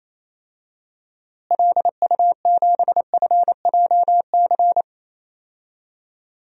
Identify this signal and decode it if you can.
Morse code, heard as a keyed tone, about 28 wpm, 700 Hz. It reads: LU7FJC
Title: DXCC-100 Call Signs - 6 Characters - Repeated 3x - 28wpm